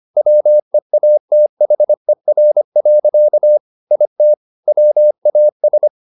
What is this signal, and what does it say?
Morse code, heard as a keyed tone, about 25 wpm, 600 Hz. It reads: WEATHER. IT WAS
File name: the old man and the sea part 1 25wpm